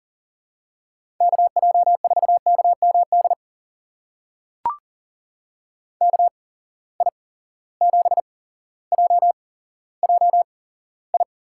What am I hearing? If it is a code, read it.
Morse code, heard as a keyed tone, about 40 wpm, spaced as for 17 wpm, 700 Hz. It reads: KJ4KMD KI7JJI
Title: USA Call Signs - 6 Characters - 8x Character Spacing - 40wpm